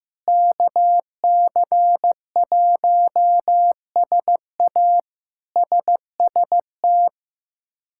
Morse code, heard as a keyed tone, about 16 wpm, 700 Hz. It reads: KC1SA SST
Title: K1USN SST QSO Practice - 15wpm